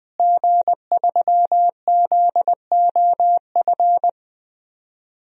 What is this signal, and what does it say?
Morse code, heard as a keyed tone, about 19 wpm, 700 Hz. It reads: G3ZOF
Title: DXCC-100 Call Signs - 5 Characters - Repeated 2x - 20wpm